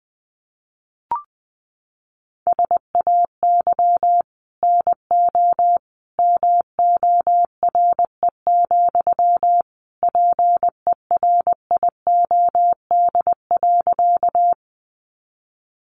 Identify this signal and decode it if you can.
Morse code, heard as a keyed tone, about 20 wpm, 700 Hz. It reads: SAY NO MORE, PERIOD.
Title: Sentences from Top 800 Words 20wpm